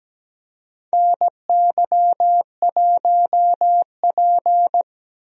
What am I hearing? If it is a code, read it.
Morse code, heard as a keyed tone, about 17 wpm, 700 Hz. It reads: NY1P
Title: USA Call Signs - 4 Characters - Repeated 2x - 17wpm